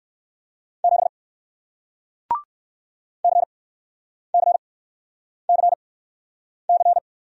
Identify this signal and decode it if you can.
Morse code, heard as a keyed tone, about 60 wpm, spaced as for 9 wpm, 700 Hz. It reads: 7 CCCC